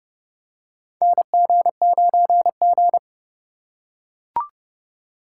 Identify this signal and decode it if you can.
Morse code, heard as a keyed tone, about 32 wpm, 700 Hz. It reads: NG9G
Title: USA Call Signs - 2x Character Spacing - 30wpm